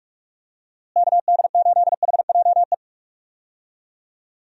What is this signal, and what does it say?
Morse code, heard as a keyed tone, about 41 wpm, 700 Hz. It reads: KD8HJE
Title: USA Call Signs - 6 Characters - Repeated 2x - 45wpm